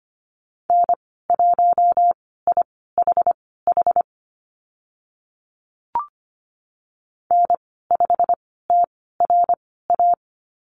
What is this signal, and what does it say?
Morse code, heard as a keyed tone, about 25 wpm, spaced as for 16 wpm, 700 Hz. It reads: N1IHH N5TRA
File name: Lesson 15g - Number 5 - Call Signs 25wpm